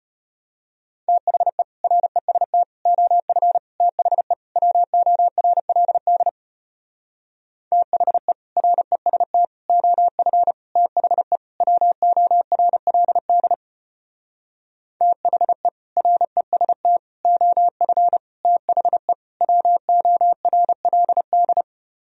Morse code, heard as a keyed tone, about 33 wpm, 700 Hz. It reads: THE REST OF THE WORLD THE REST OF THE WORLD THE REST OF THE WORLD